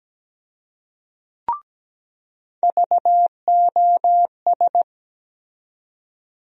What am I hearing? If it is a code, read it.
Morse code, heard as a keyed tone, about 17 wpm, 700 Hz. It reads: VOS
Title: USA Call Sign Postfixes - 3 Letters 17wpm